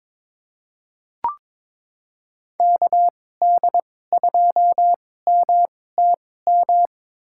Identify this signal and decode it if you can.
Morse code, heard as a keyed tone, about 21 wpm, spaced as for 16 wpm, 700 Hz. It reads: KD2MTM
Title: USA Call Signs - 2x Character Spacing - 22wpm